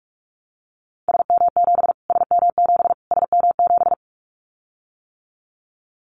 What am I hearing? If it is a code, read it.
Morse code, heard as a keyed tone, about 44 wpm, 700 Hz. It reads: SM7 SM7 SM7
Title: Callsign Prefixes for Top 25 DXCC - Repeated 3x - 45wpm